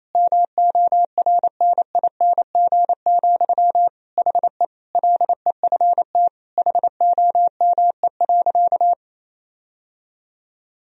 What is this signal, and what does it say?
Morse code, heard as a keyed tone, about 28 wpm, 700 Hz. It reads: MORNING, HE LEFT HOME.
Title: Sentences from Top 1000 Words - Review Part 4 - 28wpm